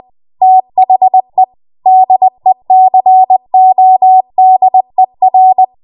Morse code, heard as a keyed tone, about 20 wpm, 750 Hz. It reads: THE DECODER
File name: Morse / 20WPM